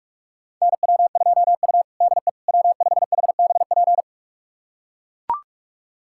Morse code, heard as a keyed tone, about 41 wpm, 700 Hz. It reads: NW2U DE W5HBP
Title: USA Callsign DE Callsign 45wpm